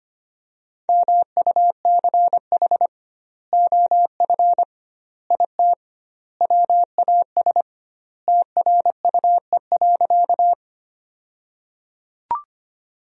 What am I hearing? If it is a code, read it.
Morse code, heard as a keyed tone, about 25 wpm, 700 Hz. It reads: MUCH OF IT WAS TRUE.
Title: 2x Word-Spacing - Sentences from Top 400 Words - 25wpm